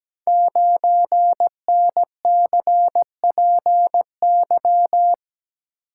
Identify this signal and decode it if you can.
Morse code, heard as a keyed tone, about 17 wpm, 700 Hz. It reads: 9NCPY
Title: Code Groups - 5 Letter:Numbers - 17wpm